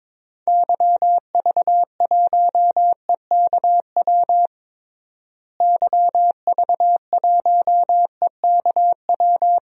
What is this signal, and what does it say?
Morse code, heard as a keyed tone, about 22 wpm, 700 Hz. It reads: YV1EKW YV1EKW
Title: DXCC-100 Call Signs - 6 Characters - Repeated 3x - 22wpm